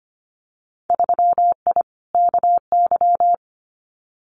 Morse code, heard as a keyed tone, about 25 wpm, 700 Hz. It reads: 3I KY
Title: Winter Field Day Exchanges 25wpm